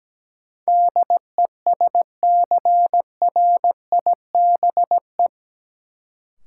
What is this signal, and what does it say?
Morse code, heard as a keyed tone, about 17 wpm, 700 Hz. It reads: DESCRIBE
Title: Sentences from Top 1000 Words - New Word Drill - 17wpm